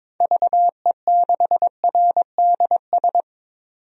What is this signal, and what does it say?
Morse code, heard as a keyed tone, about 22 wpm, 700 Hz. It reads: VE6RDS
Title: DXCC-100 Call Signs - 6 Characters - 22wpm